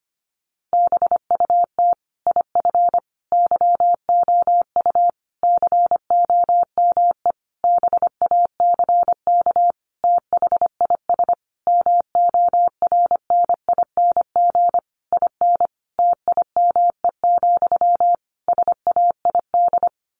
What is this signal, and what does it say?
Morse code, heard as a keyed tone, about 25 wpm, 700 Hz. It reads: BUT IF YOU COME BACK THIS MORNING IN TIME, SAID
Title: A Sound Of Thunder 25wpm